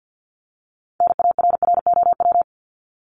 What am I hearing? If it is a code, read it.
Morse code, heard as a keyed tone, about 50 wpm, 700 Hz. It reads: NARROW
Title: Top 1501-1750 Words 50wpm